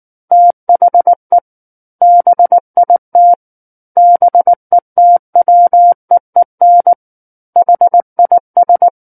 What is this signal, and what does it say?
Morse code, heard as a keyed tone, about 19 wpm, 700 Hz. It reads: THE BIT BETWEEN HIS
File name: Book One / 10-Chapter 10 - 19 WPM